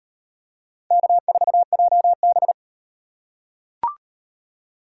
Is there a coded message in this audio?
Yes: Morse code from 0.9 to 2.5 s, heard as a keyed tone, about 38 wpm, 700 Hz. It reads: K4JB